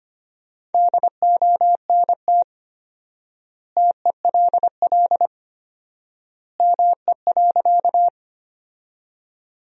Morse code, heard as a keyed tone, about 25 wpm, 700 Hz. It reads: DONT TELL ME.
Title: 4x Word-Spacing - Sentences from Top 300 Words - 25wpm